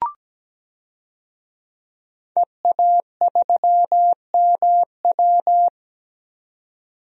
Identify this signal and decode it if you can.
Morse code, heard as a keyed tone, about 17 wpm, 700 Hz. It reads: EA3MW
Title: DXCC-100 Call Signs - 5 Characters - 17wpm